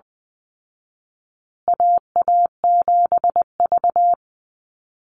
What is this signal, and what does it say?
Morse code, heard as a keyed tone, about 20 wpm, 700 Hz. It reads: AA7V
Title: Call Signs Repeated 2x - 20wpm